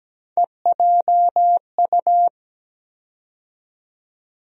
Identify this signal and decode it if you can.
Morse code, heard as a keyed tone, about 17 wpm, 700 Hz. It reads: EJU
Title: USA Call Sign Postfixes - 3 Letters 17wpm